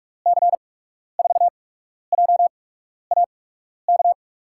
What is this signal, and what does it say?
Morse code, heard as a keyed tone, about 46 wpm, spaced as for 12 wpm, 700 Hz. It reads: C4JAK
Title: USA Call Signs - 6 Characters - 8x Character Spacing - 45wpm